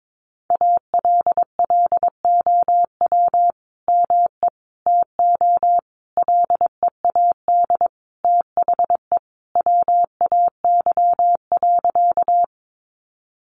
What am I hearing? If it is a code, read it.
Morse code, heard as a keyed tone, about 22 wpm, 700 Hz. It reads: ALLOW ME TO LEAD THE WAY.
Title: Sentences from Top 700 Words 22wpm